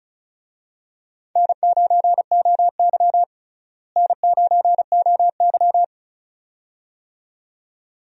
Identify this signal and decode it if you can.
Morse code, heard as a keyed tone, about 34 wpm, 700 Hz. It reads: N9OY N9OY
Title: USA Call Signs - 4 Characters - Repeated 2x - 35wpm